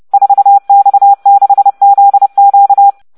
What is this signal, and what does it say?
Morse code, heard as a keyed tone, about 30 wpm, 800 Hz. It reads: 4X6ZQ